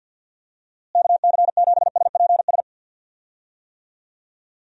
Morse code, heard as a keyed tone, about 46 wpm, 700 Hz. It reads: KC7SPS
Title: USA Call Signs - 6 Characters - Repeated 3x - 50wpm